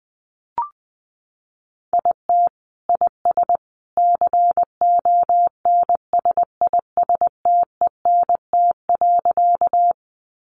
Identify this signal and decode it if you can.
Morse code, heard as a keyed tone, about 20 wpm, 700 Hz. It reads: IT IS CONSISTENT.
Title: Sentences from Top 1500 Words 20wpm